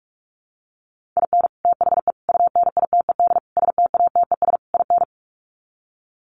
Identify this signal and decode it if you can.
Morse code, heard as a keyed tone, about 44 wpm, 700 Hz. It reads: IN THE UNITED STATES IN
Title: Sets of 5 Words 45wpm